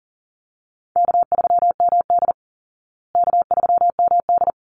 Morse code, heard as a keyed tone, about 40 wpm, 700 Hz. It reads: K3MD K3MD
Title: Call Signs Repeated 3x - 40wpm